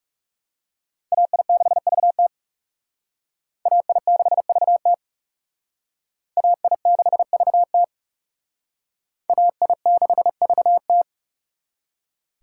Morse code, heard as a keyed tone, about 35 wpm, 700 Hz. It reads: AI6VT AI6VT AI6VT AI6VT
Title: Speed Racing - USA 2x2 Call Signs 30wpm